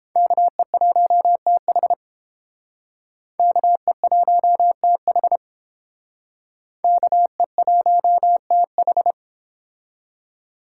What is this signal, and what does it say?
Morse code, heard as a keyed tone, about 30 wpm, 700 Hz. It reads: KE1TH KE1TH KE1TH